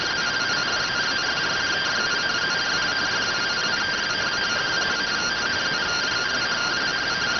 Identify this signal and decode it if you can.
Two-tone teleprinter signal, about 170 Hz shift.